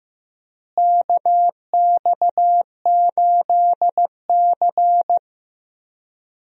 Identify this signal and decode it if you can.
Morse code, heard as a keyed tone, about 15 wpm, 700 Hz. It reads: KX8C
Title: USA Call Signs - 4 Characters - Repeated 2x - 15wpm